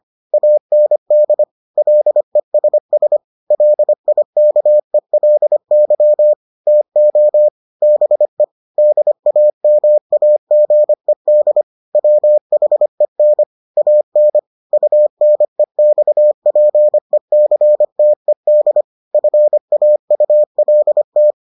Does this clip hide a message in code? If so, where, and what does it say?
Yes: Morse code from 0.3 to 21.3 s, heard as a keyed tone, about 25 wpm, 600 Hz. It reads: AND LESS LIKELY TO BE DAMAGED WHEN AN UNEXPECTED FAULT